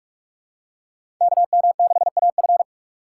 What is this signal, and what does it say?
Morse code, heard as a keyed tone, about 46 wpm, 700 Hz. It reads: KM6AF
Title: USA Call Signs - 5 Characters - 2x Character Spacing - 45wpm